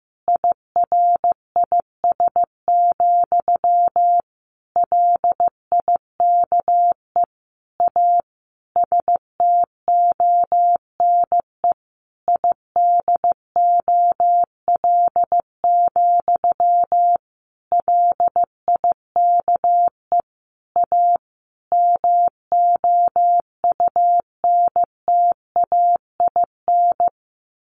Morse code, heard as a keyed tone, about 15 wpm, 700 Hz. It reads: IRIS, LIKE A STONE IDOL, LIKE A MOUNTAIN
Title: A Sound Of Thunder 15wpm